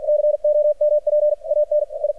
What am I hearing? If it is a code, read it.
Morse code, heard as a keyed tone, about 53 wpm, spaced as for 41 wpm, 600 Hz. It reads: COMPANI
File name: archived / CRSnet